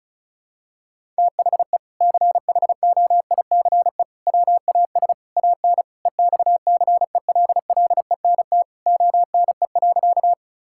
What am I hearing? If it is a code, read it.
Morse code, heard as a keyed tone, about 34 wpm, 700 Hz. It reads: THE CHOICE WAS AN EXCELLENT ONE.
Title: Sentences from Top 1000 Words - Review Part 6 - 35wpm